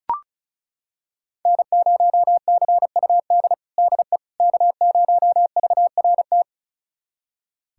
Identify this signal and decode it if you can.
Morse code, heard as a keyed tone, about 36 wpm, 700 Hz. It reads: N0CUD DE K0VRT
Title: USA Callsign DE Callsign - 5 Characters - 35wpm